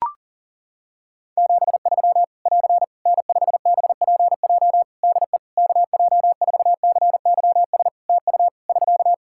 Morse code, heard as a keyed tone, about 40 wpm, 700 Hz. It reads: 73 <AR> N5BPJ DE KJ4CYS TU <SK>
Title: Practice QSO - Exchange 4 - Ending - 40wpm